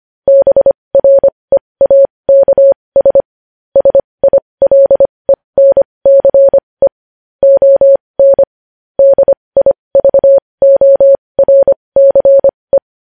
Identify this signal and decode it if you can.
Morse code, heard as a keyed tone, about 25 wpm, spaced as for 20 wpm, 550 Hz. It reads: BREAKS SILENCE ON DIVORCE
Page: News Headlines in Morse Code at 20 WPM